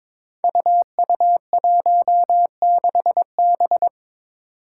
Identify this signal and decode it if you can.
Morse code, heard as a keyed tone, about 22 wpm, 700 Hz. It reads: UU16B
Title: Five Challenging Letter-Number Combinations 22wpm